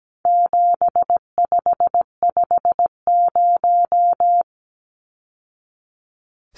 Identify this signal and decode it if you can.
Morse code, heard as a keyed tone, about 17 wpm, 700 Hz. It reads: 7550
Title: Numbers - 4-Digit 17wpm